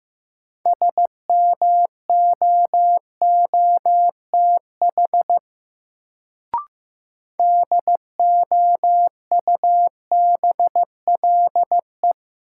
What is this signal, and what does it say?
Morse code, heard as a keyed tone, about 15 wpm, 700 Hz. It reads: SMOOTH DOUBLE
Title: Sentences from Top 1400 Words - New Word Drill - 15wpm